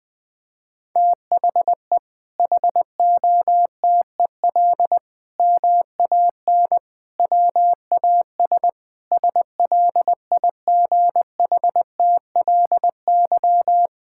Morse code, heard as a keyed tone, about 20 wpm, 700 Hz. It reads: THE HOTEL MAN WAS SLIGHTLY
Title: Sentences from Top 1600 Words 20wpm